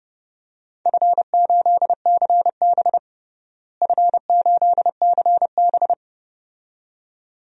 Morse code, heard as a keyed tone, about 30 wpm, 700 Hz. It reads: F8CB F8CB
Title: DXCC-100 Call Signs - 4 Characters - Repeated 2x - 30wpm